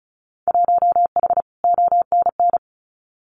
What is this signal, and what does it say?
Morse code, heard as a keyed tone, about 34 wpm, 700 Hz. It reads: 1H ONN
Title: Winter Field Day Exchanges 35wpm